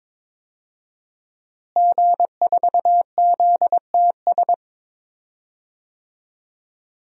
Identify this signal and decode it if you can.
Morse code, heard as a keyed tone, about 22 wpm, 700 Hz. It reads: G4ZTS
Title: DXCC-100 Call Signs - 5 Characters - Repeated 3x - 22wpm